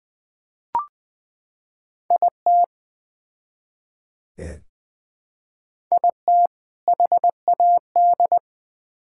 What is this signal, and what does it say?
Morse code, heard as a keyed tone, about 21 wpm, 700 Hz. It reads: IT IT HAD
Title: Build A Phrase - 3-Word Phrases - 20wpm